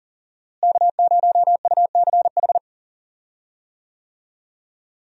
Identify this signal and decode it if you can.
Morse code, heard as a keyed tone, about 40 wpm, 700 Hz. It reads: K0UCH
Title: USA Call Signs - 5 Characters - Repeated 3x - 40wpm